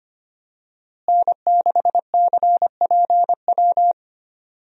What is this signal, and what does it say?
Morse code, heard as a keyed tone, about 25 wpm, 700 Hz. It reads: N6CPW
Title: USA Call Signs - 5 Characters - Repeated 2x - 25wpm